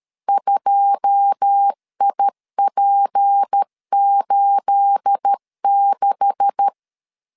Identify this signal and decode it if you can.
Morse code, heard as a keyed tone, about 13 wpm, 800 Hz. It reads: 2IP86
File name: CW / DET / 1 9 1500C S A 12wpm 3800